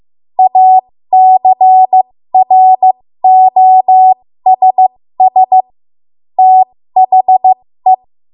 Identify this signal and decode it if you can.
Morse code, heard as a keyed tone, about 15 wpm, spaced as for 13 wpm, 750 Hz. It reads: ACROSS THE
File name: Morse / 13WPM